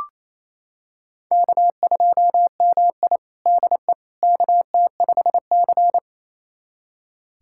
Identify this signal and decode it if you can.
Morse code, heard as a keyed tone, about 28 wpm, 700 Hz. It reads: K2MI DE KT5C
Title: USA Callsign DE Callsign - 4 Characters - 28wpm